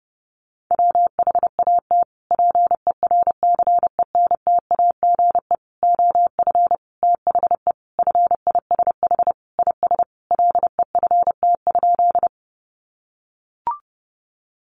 Morse code, heard as a keyed tone, about 30 wpm, 700 Hz. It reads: WHAT PERCENTAGE OF THE FISH IS LEFT?
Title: Sentences from Top 1300 Words 30wpm